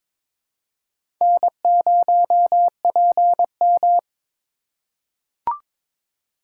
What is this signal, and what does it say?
Morse code, heard as a keyed tone, about 22 wpm, 700 Hz. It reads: N0PM
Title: USA Call Signs - 4x Character Spacing - 22wpm